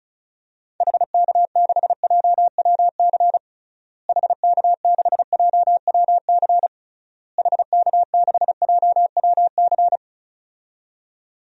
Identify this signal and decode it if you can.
Morse code, heard as a keyed tone, about 36 wpm, 700 Hz. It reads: HK6JWC HK6JWC HK6JWC
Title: DXCC-100 Call Signs - 6 Characters - Repeated 3x - 35wpm